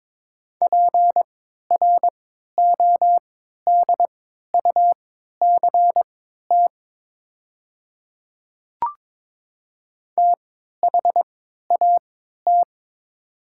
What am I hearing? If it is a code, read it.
Morse code, heard as a keyed tone, about 22 wpm, spaced as for 12 wpm, 700 Hz. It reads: PRODUCT THAT
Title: 3x Character Spacing - Top 500 Words - 22wpm